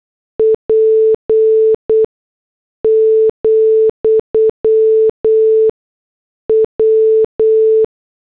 Morse code, heard as a keyed tone, about 8 wpm, spaced as for 6 wpm, 440 Hz. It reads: P, W